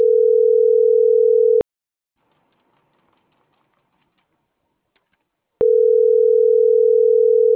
Ringback tone (North America), 440 + 480 Hz.
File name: CENTRAL MAERSK / out gabrielle.melendez